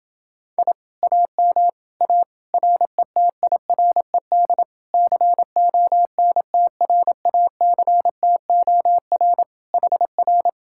Morse code, heard as a keyed tone, about 28 wpm, 700 Hz. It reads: I AM A RETIRED CONTRACTOR HR